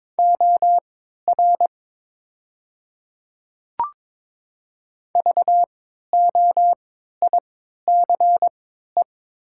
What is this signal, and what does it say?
Morse code, heard as a keyed tone, about 22 wpm, spaced as for 12 wpm, 690 Hz. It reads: OR VOICE
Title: 3x Character Spacing - Top 500 Words - 22wpm